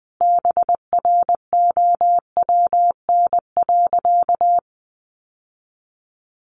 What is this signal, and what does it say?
Morse code, heard as a keyed tone, about 20 wpm, 700 Hz. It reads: BROWN.
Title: Sentences from Top 1200 Words 20wpm